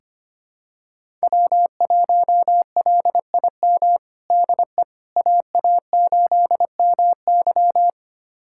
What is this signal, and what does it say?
Morse code, heard as a keyed tone, about 25 wpm, 700 Hz. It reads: W1LIM DE AA8MY
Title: USA Callsign DE Callsign - 5 Characters - 25wpm